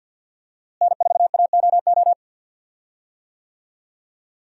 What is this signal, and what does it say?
Morse code, heard as a keyed tone, about 51 wpm, 700 Hz. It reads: N4AOO
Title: USA Call Signs - 5 Characters - Repeated 2x - 50wpm